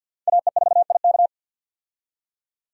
Morse code, heard as a keyed tone, about 50 wpm, 700 Hz. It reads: AE4IK
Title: USA Call Signs - 5 Characters - 50wpm